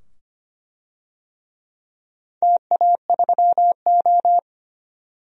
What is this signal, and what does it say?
Morse code, heard as a keyed tone, about 25 wpm, 700 Hz. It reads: TA3O
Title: DXCC-100 Call Signs - 4 Characters - 25wpm